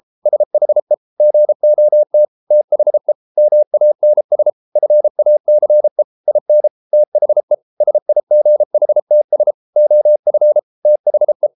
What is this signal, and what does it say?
Morse code, heard as a keyed tone, about 33 wpm, 600 Hz. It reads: SHE GOT THE MANS FACE IN THE SIGHTS OF THE